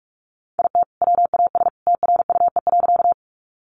Morse code, heard as a keyed tone, about 46 wpm, 700 Hz. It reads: IT WAS TRUE.
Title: Sentences from Top 400 Words 45wpm